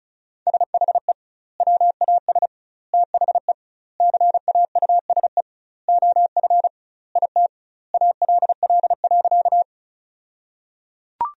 Morse code, heard as a keyed tone, about 36 wpm, 700 Hz. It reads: SHE WAS THE CAUSE OF IT ALL.
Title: 2x Word-Spacing - Sentences from Top 200 Words - 35wpm